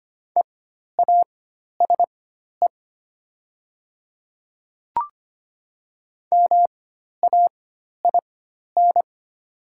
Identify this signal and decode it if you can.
Morse code, heard as a keyed tone, about 25 wpm, spaced as for 12 wpm, 700 Hz. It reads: EASE MAIN
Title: 4x Character Spacing - Top 500 Words - 25wpm